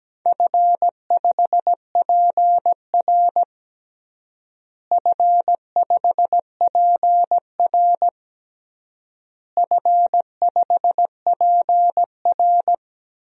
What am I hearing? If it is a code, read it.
Morse code, heard as a keyed tone, about 17 wpm, 700 Hz. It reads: F5PR F5PR F5PR
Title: DXCC-100 Call Signs - 4 Characters - Repeated 3x - 17wpm